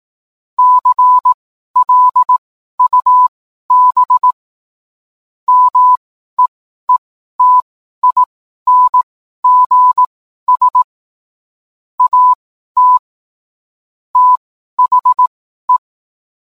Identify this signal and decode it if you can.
Morse code, heard as a keyed tone, about 18 wpm, spaced as for 12 wpm, 1000 Hz. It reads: CLUB MEETINGS AT THE